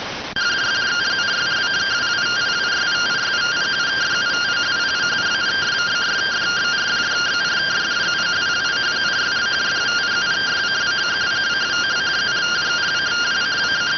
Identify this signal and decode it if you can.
Two-tone teleprinter signal, about 170 Hz shift.